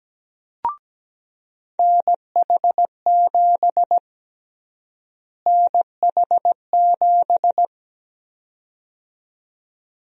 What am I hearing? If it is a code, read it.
Morse code, heard as a keyed tone, about 17 wpm, 700 Hz. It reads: NH7 NH7
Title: USA Call Sign Prefixes - Repeated 2x - 17wpm